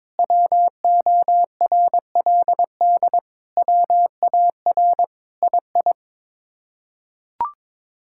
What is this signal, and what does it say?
Morse code, heard as a keyed tone, about 21 wpm, 700 Hz. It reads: WORLD WAR II
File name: practice / 22wpm / Sets of 7 Words - 22wpm